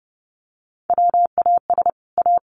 Morse code, heard as a keyed tone, about 30 wpm, 700 Hz. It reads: WAS A
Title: Sets of 5 Words 30wpm